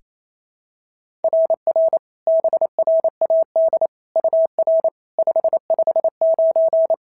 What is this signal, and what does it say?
Morse code, heard as a keyed tone, about 25 wpm, 650 Hz. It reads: RR BRAD UR 559
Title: SOTA QSO Practice 28wpm